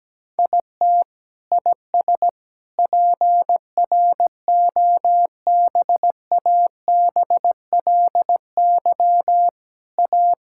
Morse code, heard as a keyed tone, about 17 wpm, 700 Hz. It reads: IT IS PROBABLY A
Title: Sentences from Top 1000 Words - Review Part 6 - 17wpm